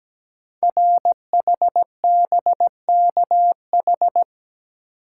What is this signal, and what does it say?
Morse code, heard as a keyed tone, about 17 wpm, 700 Hz. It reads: RHBKH
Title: Code Groups - 5 Letters - 17wpm